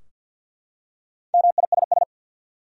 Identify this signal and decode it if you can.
Morse code, heard as a keyed tone, about 52 wpm, 700 Hz. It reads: MISS
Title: Four Letter Words - 50wpm